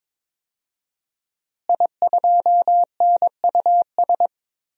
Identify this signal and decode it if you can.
Morse code, heard as a keyed tone, about 22 wpm, 700 Hz. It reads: I2NUS
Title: DXCC-100 Call Signs - 5 Characters - Repeated 2x - 22wpm